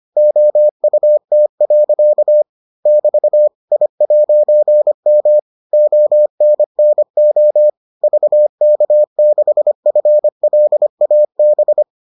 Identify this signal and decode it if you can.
Morse code, heard as a keyed tone, about 25 wpm, 600 Hz. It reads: OUT. <BT> I'M ONNO VK6FLAB